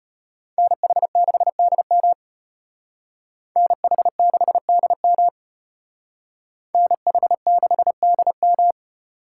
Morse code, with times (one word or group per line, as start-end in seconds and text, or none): 0.6-2.1 NH6DM
3.6-5.3 NH6DM
6.7-8.7 NH6DM